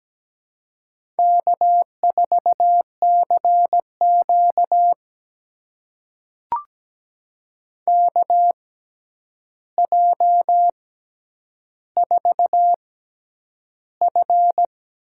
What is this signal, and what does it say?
Morse code, heard as a keyed tone, about 17 wpm, spaced as for 9 wpm, 700 Hz. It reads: K4CQ KJ4F
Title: USA Call Signs - 6x Character Spacing - 17wpm